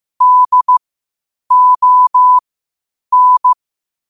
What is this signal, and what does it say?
Morse code, heard as a keyed tone, about 15 wpm, spaced as for 8 wpm, 1000 Hz. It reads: DON
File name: qotdfiles / 8wpm